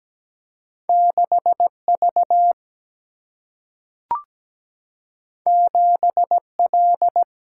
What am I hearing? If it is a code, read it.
Morse code, heard as a keyed tone, about 17 wpm, 700 Hz. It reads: 6V 7L
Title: Single-Letter Single-Number 17wpm